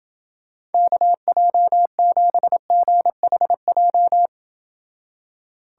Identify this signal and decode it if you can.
Morse code, heard as a keyed tone, about 29 wpm, 700 Hz. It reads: KJ7GHJ